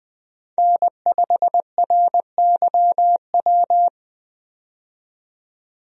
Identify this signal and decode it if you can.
Morse code, heard as a keyed tone, about 20 wpm, 700 Hz. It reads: N5RYW
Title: USA Call Signs - 5 Characters - 20wpm